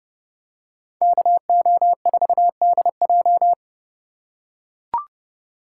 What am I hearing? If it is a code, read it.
Morse code, heard as a keyed tone, about 29 wpm, 700 Hz. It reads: KO4DJ